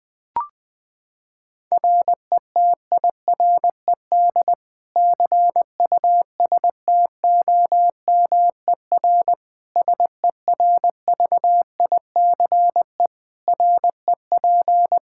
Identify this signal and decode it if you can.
Morse code, heard as a keyed tone, about 20 wpm, 700 Hz. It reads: RETIRED CUSTOMER SERVICE REP
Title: Practice QSO - Exchange 3 - More Info - 20wpm